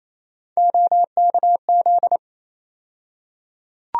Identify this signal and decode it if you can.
Morse code, heard as a keyed tone, about 28 wpm, 700 Hz. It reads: OKZ
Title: USA Call Sign Postfixes - 3 Letters 28wpm